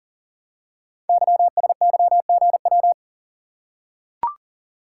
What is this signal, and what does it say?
Morse code, heard as a keyed tone, about 39 wpm, 700 Hz. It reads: YSYGW
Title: Code Groups - 5 Letters - 40wpm